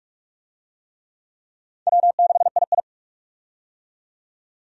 Morse code, heard as a keyed tone, about 42 wpm, 700 Hz. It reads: W6II
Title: USA Call Signs - 4 Characters - 45wpm